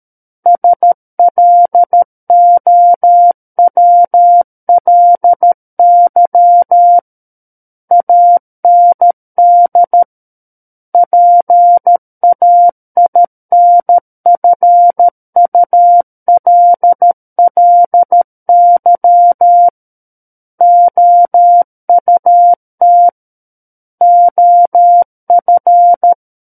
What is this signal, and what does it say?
Morse code, heard as a keyed tone, about 13 wpm, 700 Hz. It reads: SLOWLY AND PAINFULLY OUT OF
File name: Book One / 04-Chapter 4 - 13 WPM